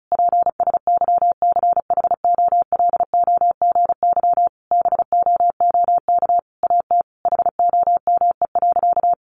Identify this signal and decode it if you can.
Morse code, heard as a keyed tone, about 34 wpm, 700 Hz. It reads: PSYCHOLOGY BOOK AT HOME.
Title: Sentences from Top 1400 Words 35wpm